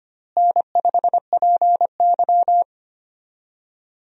Morse code, heard as a keyed tone, about 25 wpm, 700 Hz. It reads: N5PY
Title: USA Call Signs - 4 Characters - Repeated 3x - 25wpm